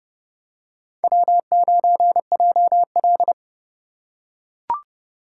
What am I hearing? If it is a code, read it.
Morse code, heard as a keyed tone, about 29 wpm, 700 Hz. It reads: W9JL